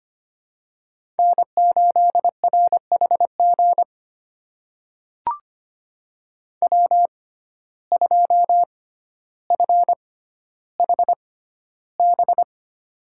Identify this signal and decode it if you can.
Morse code, heard as a keyed tone, about 25 wpm, spaced as for 13 wpm, 700 Hz. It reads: N8RHG W2FHB
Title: USA Call Signs - 5 Characters - 6x Character Spacing - 25wpm